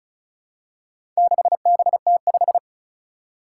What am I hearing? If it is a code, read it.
Morse code, heard as a keyed tone, about 36 wpm, 700 Hz. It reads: 6BT5
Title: Four Challenging Letter-Number Combinations 35wpm